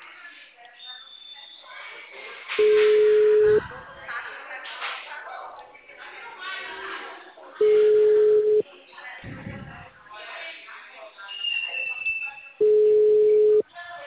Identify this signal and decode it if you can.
Ringback tone (Europe, 425 Hz), with speech.